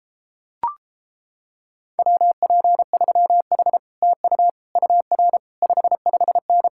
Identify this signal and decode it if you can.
Morse code, heard as a keyed tone, about 31 wpm, 700 Hz. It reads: WP3H TU UR 55N